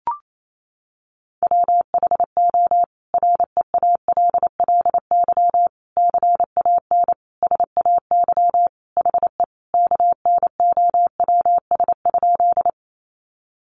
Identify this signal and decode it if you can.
Morse code, heard as a keyed tone, about 28 wpm, 700 Hz. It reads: WHO REALLY CAN SAY HE KNOWS?
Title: A Sound Of Thunder 28wpm